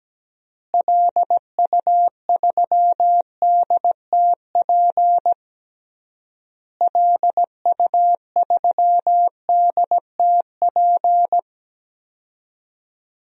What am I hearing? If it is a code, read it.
Morse code, heard as a keyed tone, about 17 wpm, 700 Hz. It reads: LU3DTP LU3DTP
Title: DXCC-100 Call Signs - 6 Characters - Repeated 3x - 17wpm